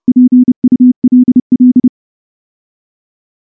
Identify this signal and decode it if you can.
Morse code, heard as a keyed tone, about 30 wpm, 250 Hz. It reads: PULL
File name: morse / morse-training-30WPM